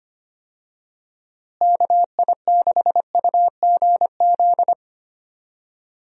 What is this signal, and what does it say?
Morse code, heard as a keyed tone, about 25 wpm, 700 Hz. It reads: KI6UGZ
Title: USA Call Signs - 6 Characters - Repeated 2x - 25wpm